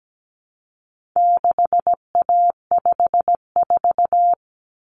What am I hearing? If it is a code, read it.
Morse code, heard as a keyed tone, about 17 wpm, 700 Hz. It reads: 6A54
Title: Common Cut Numbers - 4-Digit 17wpm